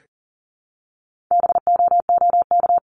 Morse code, heard as a keyed tone, about 40 wpm, 700 Hz. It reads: BOOK
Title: ES:S Suffix - Top 100 Words with Base Words 40wpm